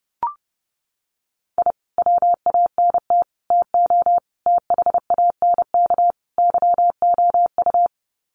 Morse code, heard as a keyed tone, about 30 wpm, 700 Hz. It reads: I WANT TO THANK YOU
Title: Sets of 5 Words Encore 30wpm